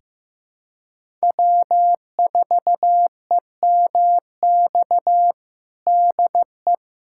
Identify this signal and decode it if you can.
Morse code, heard as a keyed tone, about 16 wpm, 700 Hz. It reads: W4EMX DE
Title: USA Callsign DE Callsign - 5 Characters - 15wpm